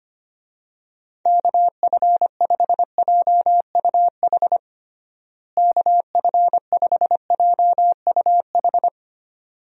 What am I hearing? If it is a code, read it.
Morse code, heard as a keyed tone, about 25 wpm, 700 Hz. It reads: KF5JUH KF5JUH
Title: USA Call Signs - 6 Characters - Repeated 2x - 25wpm